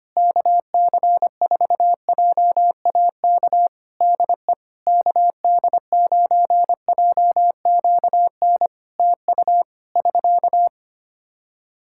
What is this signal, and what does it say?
Morse code, heard as a keyed tone, about 25 wpm, 700 Hz. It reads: KC4JAK DE KD9JQN TU <SK>
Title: Practice QSO - Exchange 4 - Ending - 25wpm